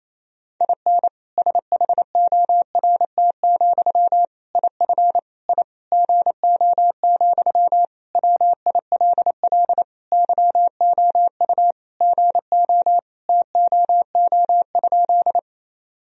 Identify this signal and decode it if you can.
Morse code, heard as a keyed tone, about 28 wpm, 700 Hz. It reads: IN SHORT, IF I GO, WILL YOU GO TOO?
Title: Sentences from Top 1000 Words - Review Part 2 - 28wpm